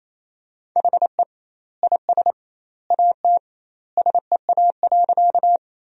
Morse code, heard as a keyed tone, about 28 wpm, 700 Hz. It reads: HE IS AT SEA.
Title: 2x Word-Spacing - Sentences from Top 300 Words - 28wpm